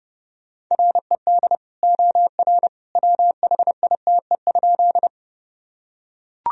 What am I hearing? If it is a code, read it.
Morse code, heard as a keyed tone, about 30 wpm, 700 Hz. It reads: RED OR WHITE?
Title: Sentences from Top 1000 Words - Review Part 2 - 30wpm